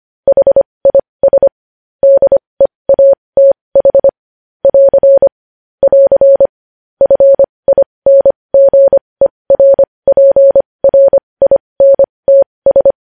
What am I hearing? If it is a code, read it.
Morse code, heard as a keyed tone, about 25 wpm, spaced as for 20 wpm, 550 Hz. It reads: HIS DEATH <AR> <AR> FINGERPRINTS
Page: news-headlines-in-morse-code-at-20-wpm